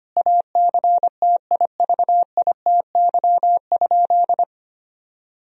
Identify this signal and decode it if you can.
Morse code, heard as a keyed tone, about 25 wpm, 700 Hz. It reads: ACTIVITY?
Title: Sentences from Top 1000 Words - Review Part 4 - 25wpm